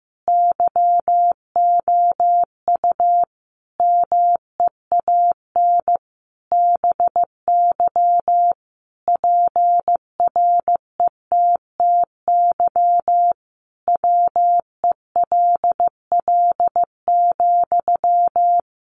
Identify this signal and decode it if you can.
Morse code, heard as a keyed tone, about 14 wpm, 700 Hz. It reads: YOU MEAN BY PRETTY WELL,
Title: Sentences from Top 1000 Words 15wpm